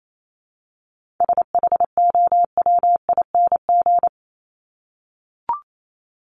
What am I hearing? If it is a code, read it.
Morse code, heard as a keyed tone, about 28 wpm, 700 Hz. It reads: SHOWING
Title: ING Suffix - Top 100 Words with Base Words 28wpm